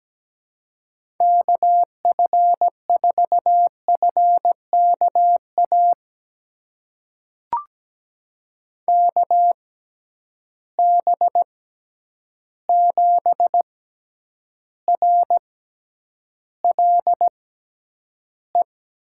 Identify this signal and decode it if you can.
Morse code, heard as a keyed tone, about 17 wpm, spaced as for 9 wpm, 700 Hz. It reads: KF4FKA KB7RLE
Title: USA Call Signs - 6 Characters - 6x Character Spacing - 17wpm